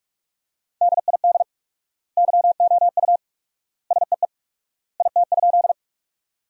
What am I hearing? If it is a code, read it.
Morse code, heard as a keyed tone, about 42 wpm, 700 Hz. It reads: DID YOU SEE IT?
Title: 4x Word-Spacing - Sentences from Top 100 Words - 45wpm